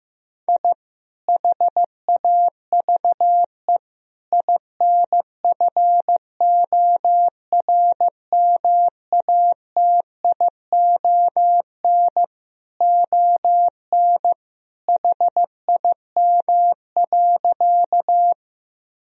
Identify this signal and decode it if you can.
Morse code, heard as a keyed tone, about 15 wpm, 700 Hz. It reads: I HAVE INFORMATION ON HIM.
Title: Sentences from Top 1000 Words - Review Part 3 - 15wpm